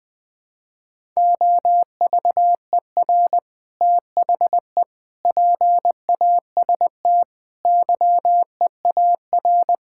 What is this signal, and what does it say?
Morse code, heard as a keyed tone, about 20 wpm, 700 Hz. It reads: OVER THE PAST YEAR
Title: Sentences with 3-Word Phrases - 20wpm